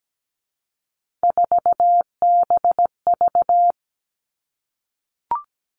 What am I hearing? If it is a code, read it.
Morse code, heard as a keyed tone, about 17 wpm, 700 Hz. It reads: 4BV
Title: Three Challenging Letter-Number Combinations 17wpm